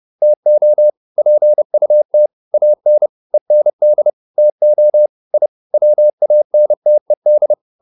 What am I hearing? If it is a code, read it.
Morse code, heard as a keyed tone, about 30 wpm, 600 Hz. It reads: TO PUT AN END TO I WANTED